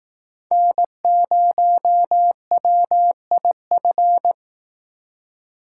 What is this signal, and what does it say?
Morse code, heard as a keyed tone, about 18 wpm, 700 Hz. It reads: N0WIF
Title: Speed Racing - USA Call Signs 15wpm